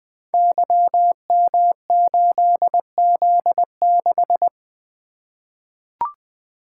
Morse code, heard as a keyed tone, about 20 wpm, 700 Hz. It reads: YM8Z6
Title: Code Groups - 5 Letter:Numbers - 20wpm